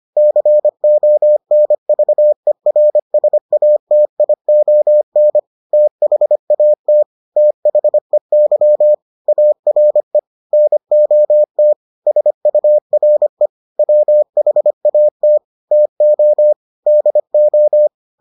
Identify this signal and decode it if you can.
Morse code, heard as a keyed tone, about 25 wpm, 600 Hz. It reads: CONVERSATION THAT THEY ARE NOT SURE WHAT TO DO